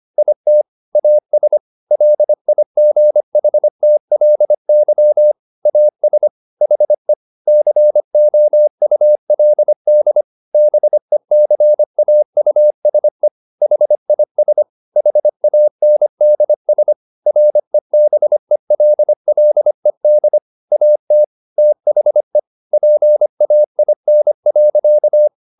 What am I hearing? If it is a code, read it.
Morse code, heard as a keyed tone, about 25 wpm, 600 Hz. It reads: IT AS LIGHTLY AS HE COULD BECAUSE HIS HANDS REBELLED AT THE PAIN.